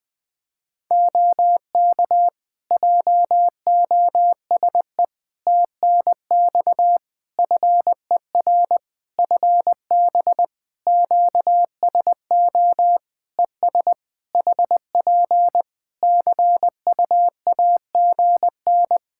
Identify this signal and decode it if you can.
Morse code, heard as a keyed tone, about 20 wpm, 700 Hz. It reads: OK JOSE TNX FER FB QSO ES HP CUAGN
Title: K1USN SST QSO Practice 20wpm